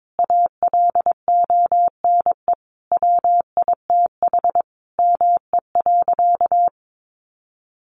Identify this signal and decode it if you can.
Morse code, heard as a keyed tone, about 22 wpm, 700 Hz. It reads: ALONE WITH ME.
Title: Sentences from Top 1400 Words 22wpm